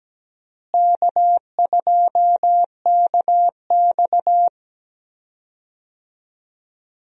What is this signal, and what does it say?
Morse code, heard as a keyed tone, about 17 wpm, 700 Hz. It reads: K2KX
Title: USA Call Signs - 4 Characters - Repeated 3x - 17wpm